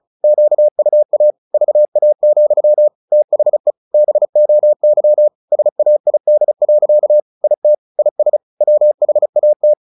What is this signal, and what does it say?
Morse code, heard as a keyed tone, about 35 wpm, 600 Hz. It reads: QUA VA, THE BOY SAID. IT IS WHAT
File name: the old man and the sea / the old man and the sea 1 35wpm